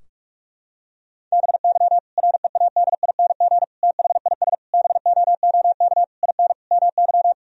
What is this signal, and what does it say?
Morse code, heard as a keyed tone, about 46 wpm, 700 Hz. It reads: BY READING THIS BOOK IN MY
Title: Sentences from Top 1000 Words - Review Part 3 - 45wpm